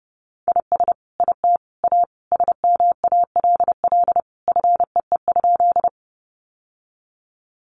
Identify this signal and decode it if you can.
Morse code, heard as a keyed tone, about 30 wpm, 700 Hz. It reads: IS IT A SMALL FEE?
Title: Sentences from Top 1500 Words 30wpm